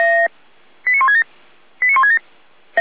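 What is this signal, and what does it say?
Five-tone selective call (ZVEI), heard as quick runs of five tones, tones of about 70 ms. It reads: 89167 89167
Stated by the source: ZVEI five-tone call 89164 received on SDR, logged by the Feuerwehr Orlamünde (f85360) server.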